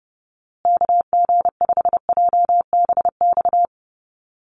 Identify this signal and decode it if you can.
Morse code, heard as a keyed tone, about 29 wpm, 700 Hz. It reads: KG5JBX
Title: USA Call Signs Repeated 3x 30wpm